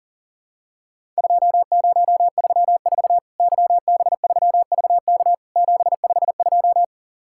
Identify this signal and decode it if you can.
Morse code, heard as a keyed tone, about 42 wpm, 700 Hz. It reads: 2034 YB3VK 752
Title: Worked All Europe DX Contest - QTC Exchange - 40wpm